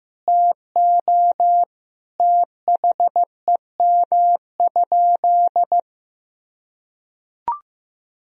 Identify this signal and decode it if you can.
Morse code, heard as a keyed tone, about 15 wpm, 700 Hz. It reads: TO THEM?
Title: Sentences from Top 1000 Words - Review Part 2 - 15wpm